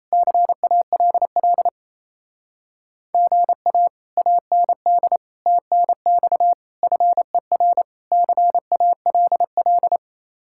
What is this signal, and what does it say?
Morse code, heard as a keyed tone, about 33 wpm, 700 Hz. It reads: CALL GA AND TNX FER CALL
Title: Speed-Racing Common QSO Phrases - 28wpm